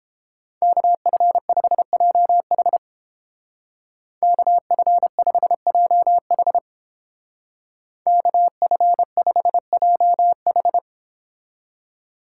Morse code, heard as a keyed tone, about 31 wpm, 700 Hz. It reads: KF5JH KF5JH KF5JH